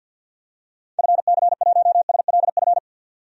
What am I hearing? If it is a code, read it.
Morse code, heard as a keyed tone, about 47 wpm, 700 Hz. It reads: FC1SLF
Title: DXCC-100 Call Signs - 6 Characters - 50wpm